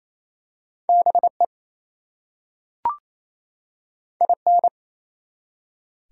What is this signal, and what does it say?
Morse code, heard as a keyed tone, about 27 wpm, 700 Hz. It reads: BE IN
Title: Two Letter Words - 28wpm